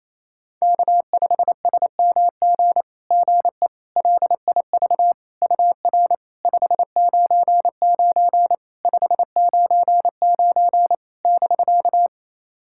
Morse code, heard as a keyed tone, about 28 wpm, 700 Hz. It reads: K5SMG GE LIV UR 599 599 <BK>
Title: SOTA QSO Practice 28wpm